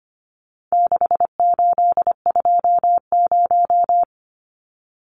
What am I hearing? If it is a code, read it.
Morse code, heard as a keyed tone, about 25 wpm, 700 Hz. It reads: 6820
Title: Numbers - 4-Digit 25wpm